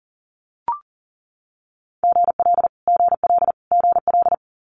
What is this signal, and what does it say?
Morse code, heard as a keyed tone, about 42 wpm, 700 Hz. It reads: GL GL GL
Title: Multi-letter Phonemes Repeated 3x 40wpm